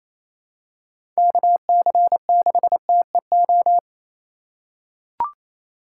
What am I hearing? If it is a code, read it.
Morse code, heard as a keyed tone, about 28 wpm, 700 Hz. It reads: KC6TEO
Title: USA Call Signs - 6 Characters - 4x Character Spacing - 28wpm